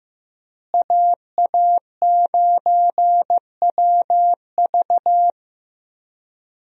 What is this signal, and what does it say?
Morse code, heard as a keyed tone, about 15 wpm, 700 Hz. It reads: AA9WV
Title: USA Call Signs - 5 Characters - Repeated 3x - 15wpm